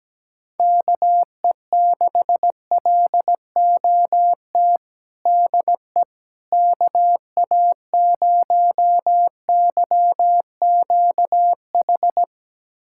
Morse code, heard as a keyed tone, about 17 wpm, 700 Hz. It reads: KE6LOT DE KA0YQH
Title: USA Callsign DE Callsign 17wpm